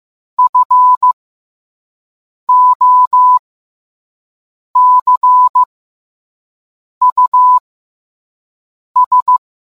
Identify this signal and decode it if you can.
Morse code, heard as a keyed tone, about 15 wpm, spaced as for 5 wpm, 1000 Hz. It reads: FOCUS